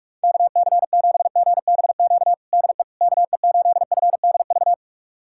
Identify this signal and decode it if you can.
Morse code, heard as a keyed tone, about 41 wpm, 700 Hz. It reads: KC7GBQ DE KE8FDV
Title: USA Callsign DE Callsign - 6 Characters - 45wpm